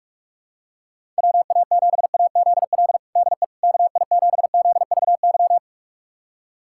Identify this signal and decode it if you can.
Morse code, heard as a keyed tone, about 46 wpm, 700 Hz. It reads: WA7AZL DE KI7ZVY
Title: USA Callsign DE Callsign - 6 Characters - 45wpm